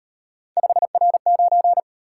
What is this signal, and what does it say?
Morse code, heard as a keyed tone, about 38 wpm, 700 Hz. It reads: 5R9